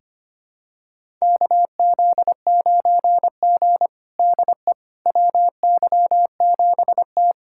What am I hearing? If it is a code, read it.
Morse code, heard as a keyed tone, about 25 wpm, 700 Hz. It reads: KZ9G DE WY7T
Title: USA Callsign DE Callsign - 4 Characters - 25wpm